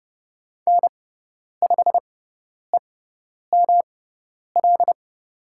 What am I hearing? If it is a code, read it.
Morse code, heard as a keyed tone, about 30 wpm, spaced as for 10 wpm, 700 Hz. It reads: N5EML
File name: Lesson 18g - Letter m - Call Signs 30wpm